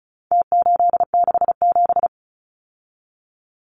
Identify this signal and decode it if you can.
Morse code, heard as a keyed tone, about 34 wpm, 700 Hz. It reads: T867
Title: Numbers - Contest Style - 4-Digit 35wpm